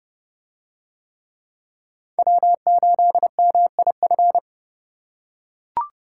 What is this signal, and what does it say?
Morse code, heard as a keyed tone, about 31 wpm, 700 Hz. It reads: W8MIF